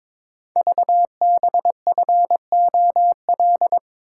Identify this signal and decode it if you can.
Morse code, heard as a keyed tone, about 23 wpm, 700 Hz. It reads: VBFOL
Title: Code Groups - 5 Letters - 22wpm